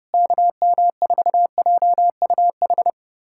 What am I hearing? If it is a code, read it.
Morse code, heard as a keyed tone, about 30 wpm, 700 Hz. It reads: KM4JUH
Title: USA Callsign DE Callsign 30wpm